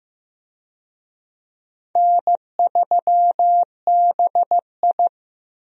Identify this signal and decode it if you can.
Morse code, heard as a keyed tone, about 15 wpm, 700 Hz. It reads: N3BI
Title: USA Call Signs - 4 Characters - Repeated 2x - 15wpm